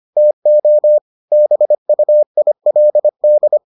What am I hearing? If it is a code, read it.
Morse code, heard as a keyed tone, about 25 wpm, 600 Hz. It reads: TO BUILD